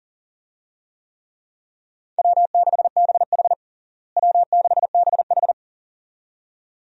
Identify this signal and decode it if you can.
Morse code, heard as a keyed tone, about 40 wpm, 700 Hz. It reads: W6BH W6BH
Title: USA Call Signs - 4 Characters - Repeated 2x - 40wpm